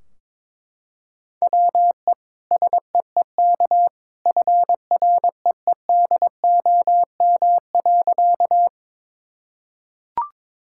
Morse code, heard as a keyed tone, about 22 wpm, 700 Hz. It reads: WE SEEK FREEDOM.
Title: Sentences from Top 1600 Words 22wpm